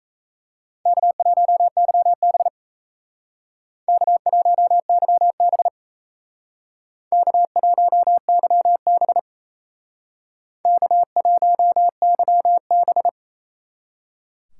Morse code, heard as a keyed tone, about 36 wpm, 700 Hz. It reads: K1YB K1YB K1YB K1YB